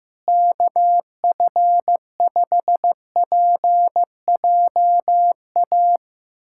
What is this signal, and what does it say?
Morse code, heard as a keyed tone, about 15 wpm, 700 Hz. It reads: KF5PJA
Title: USA Callsign DE Callsign - 6 Characters - 15wpm